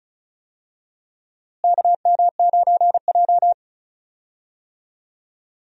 Morse code, heard as a keyed tone, about 36 wpm, 700 Hz. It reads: KM9J
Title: USA Call Signs - 4 Characters - Repeated 3x - 35wpm